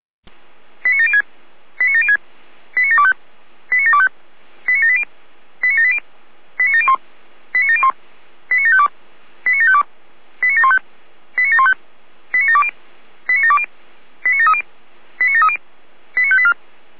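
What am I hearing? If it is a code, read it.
Five-tone selective call (ZVEI), heard as quick runs of five tones, tones of about 70 ms. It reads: 89706 89706 89725 89725 89780 89780 89701 89701 89752 89752 89715 89715 89720 89720 89730 89730 89584
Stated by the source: ZVEI five-tone selective call received by SDR.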